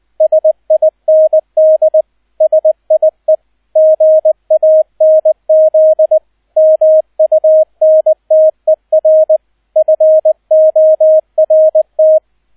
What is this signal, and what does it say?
Morse code, heard as a keyed tone, about 20 wpm, 630 Hz, with radio noise. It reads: SIND SIE GANZ MUNTER FORT